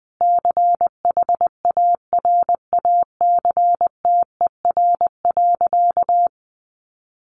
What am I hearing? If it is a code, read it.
Morse code, heard as a keyed tone, about 20 wpm, 700 Hz. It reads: CHARACTER.
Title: Sentences from Top 1200 Words 20wpm